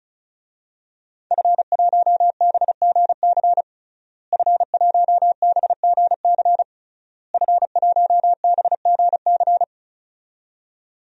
Morse code, heard as a keyed tone, about 36 wpm, 700 Hz. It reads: F1BGC F1BGC F1BGC
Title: DXCC-100 Call Signs - 5 Characters - Repeated 3x - 35wpm